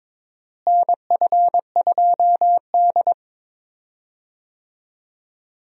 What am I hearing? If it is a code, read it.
Morse code, heard as a keyed tone, about 22 wpm, 700 Hz. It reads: NF2D
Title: USA Call Signs - 4 Characters - Repeated 3x - 22wpm